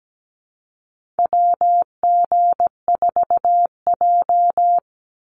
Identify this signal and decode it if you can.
Morse code, heard as a keyed tone, about 17 wpm, 700 Hz. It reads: WG4J
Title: USA Call Signs 17wpm